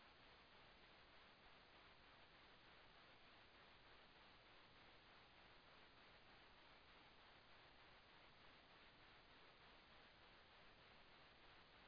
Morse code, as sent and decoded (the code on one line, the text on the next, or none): none